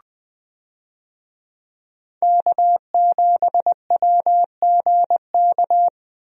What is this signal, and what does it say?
Morse code, heard as a keyed tone, about 20 wpm, 700 Hz. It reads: K7WGK
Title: USA Call Signs - 5 Characters - 20wpm